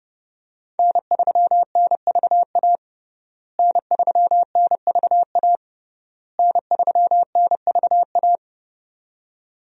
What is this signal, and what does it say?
Morse code, heard as a keyed tone, about 30 wpm, 700 Hz. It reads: N3NVA N3NVA N3NVA
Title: USA Call Signs - 5 Characters - Repeated 3x - 30wpm